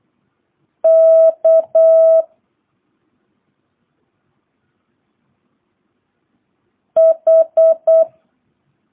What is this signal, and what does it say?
Morse code, heard as a keyed tone, about 8 wpm, 650 Hz. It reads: K H
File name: jh8cbh / 2022cw